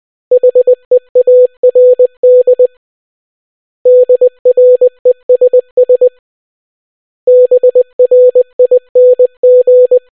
Morse code, heard as a keyed tone, about 20 wpm, 500 Hz. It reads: HEARD DRESS BRING